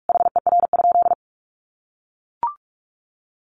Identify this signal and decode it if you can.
Morse code, heard as a keyed tone, about 44 wpm, 700 Hz. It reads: HER?
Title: Sentences from Top 1500 Words 45wpm